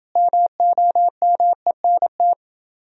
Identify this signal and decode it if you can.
Morse code, heard as a keyed tone, about 27 wpm, 700 Hz. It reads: MOMENT